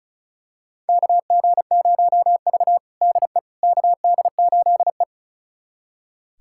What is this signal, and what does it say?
Morse code, heard as a keyed tone, about 36 wpm, 700 Hz. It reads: KG0V DE KD8E
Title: USA Callsign DE Callsign - 4 Characters - 35wpm